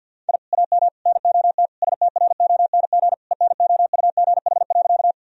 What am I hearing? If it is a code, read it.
Morse code, heard as a keyed tone, about 50 wpm, 700 Hz. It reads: I AM NOT STRONG ENOUGH.
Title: Sentences from Top 1000 Words - Review Part 3 - 50wpm